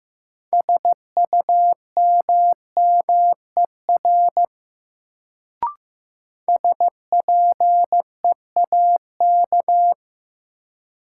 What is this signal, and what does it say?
Morse code, heard as a keyed tone, about 15 wpm, 700 Hz. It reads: SUMMER SPEAK
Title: Sentences from Top 1100 Words - New Word Drill Alt - 15wpm